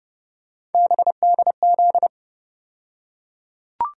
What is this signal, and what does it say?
Morse code, heard as a keyed tone, about 30 wpm, 700 Hz. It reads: BDZ
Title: USA Call Sign Postfixes - 3 Letters 30wpm